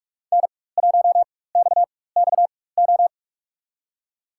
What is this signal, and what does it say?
Morse code, heard as a keyed tone, about 47 wpm, spaced as for 21 wpm, 700 Hz. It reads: N1XXO
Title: USA Call Signs - 5 Characters - 4x Character Spacing - 45wpm